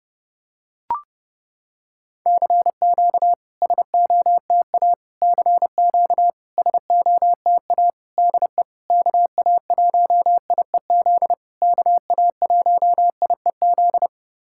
Morse code, heard as a keyed tone, about 30 wpm, 700 Hz. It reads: CQ SOTA CQ SOTA DE KA1IEZ KA1IEZ
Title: SOTA QSO Practice 30wpm